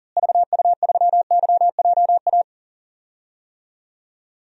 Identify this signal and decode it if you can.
Morse code, heard as a keyed tone, about 37 wpm, 700 Hz. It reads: VU3YJA
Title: DXCC-100 Call Signs - 6 Characters - Repeated 2x - 40wpm